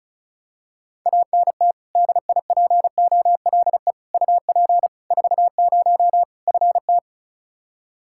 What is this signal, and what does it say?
Morse code, heard as a keyed tone, about 36 wpm, 700 Hz. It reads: ANT DIPOLE UP 40 FT
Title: Practice QSO - Exchange 2 - Info - 35wpm